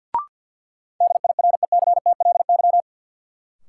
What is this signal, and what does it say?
Morse code, heard as a keyed tone, about 51 wpm, 700 Hz. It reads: DIRECTLY
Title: Sentences from Top 800 Words - New Word Drill - 50wpm